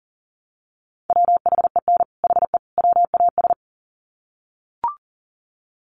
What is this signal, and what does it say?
Morse code, heard as a keyed tone, about 40 wpm, 700 Hz. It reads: WHEN HE WAS
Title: Sets of 3 Words 40wpm